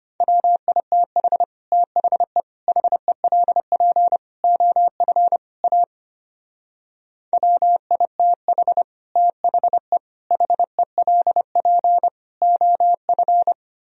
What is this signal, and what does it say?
Morse code, heard as a keyed tone, about 31 wpm, 700 Hz. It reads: WITH THE HELP OF A WITH THE HELP OF
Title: Speed-Racing - Sets of 5 Words 25wpm